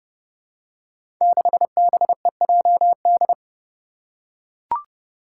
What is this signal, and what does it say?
Morse code, heard as a keyed tone, about 32 wpm, 700 Hz. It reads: 6BEJD
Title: Five Challenging Letter-Number Combinations 30wpm